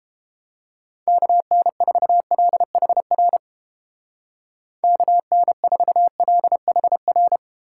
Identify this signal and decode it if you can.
Morse code, heard as a keyed tone, about 30 wpm, 700 Hz. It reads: KN4LHR KN4LHR